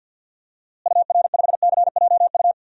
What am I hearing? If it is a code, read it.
Morse code, heard as a keyed tone, about 48 wpm, 700 Hz. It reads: UR5CJU
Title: DXCC-100 Call Signs - 6 Characters - Repeated 2x - 50wpm